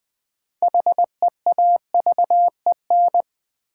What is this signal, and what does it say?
Morse code, heard as a keyed tone, about 20 wpm, 700 Hz. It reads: HEAVEN
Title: Build A Word - 3 Letters At A Time - 6-Letter Words - 20wpm